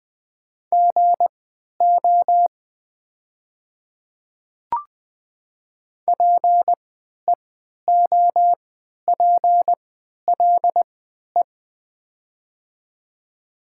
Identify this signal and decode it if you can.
Morse code, heard as a keyed tone, about 20 wpm, spaced as for 11 wpm, 700 Hz. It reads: GO PEOPLE
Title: 3x Character Spacing - Top 100 Words - 20wpm